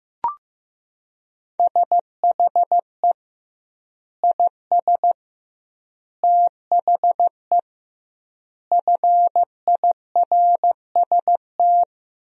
Morse code, heard as a keyed tone, about 15 wpm, 700 Hz. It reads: SHE IS THE FIRST
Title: 2x Word-Spacing - Sentences from Top 400 Words - 15wpm